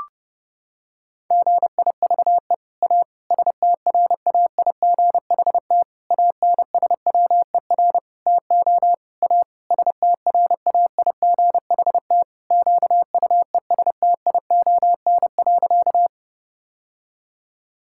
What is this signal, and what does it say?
Morse code, heard as a keyed tone, about 30 wpm, 700 Hz. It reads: GIVE A STRAIGHT ANSWER TO A STRAIGHT QUESTION.
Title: Sentences from Top 1000 Words - Review Part 6 - 30wpm